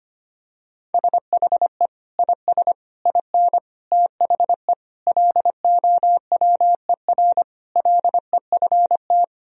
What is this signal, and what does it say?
Morse code, heard as a keyed tone, about 25 wpm, 700 Hz. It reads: SHE IS IN THE LOWER LEFT
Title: Sentences from Top 1000 Words - Review Part 4 - 25wpm